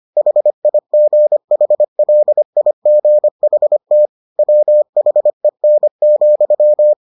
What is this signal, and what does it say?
Morse code, heard as a keyed tone, about 25 wpm, 600 Hz. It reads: HIGHLIGHT WHEN,